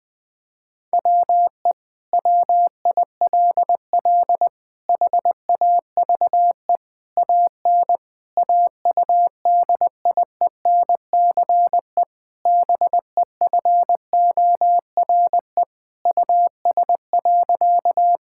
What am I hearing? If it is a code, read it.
Morse code, heard as a keyed tone, about 20 wpm, 700 Hz. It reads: WE WILL HAVE AN AUDIENCE BEFORE US.
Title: Sentences from Top 1000 Words - Review Part 5 - 20wpm